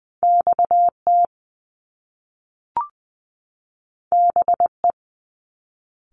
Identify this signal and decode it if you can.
Morse code, heard as a keyed tone, about 20 wpm, 700 Hz. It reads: X T B E
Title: Top Two Letter Combinations in Words 20wpm